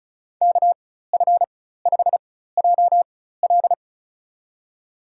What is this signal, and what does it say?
Morse code, heard as a keyed tone, about 34 wpm, spaced as for 16 wpm, 700 Hz. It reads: KF5JL
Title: USA Call Signs - 5 Characters - 4x Character Spacing - 35wpm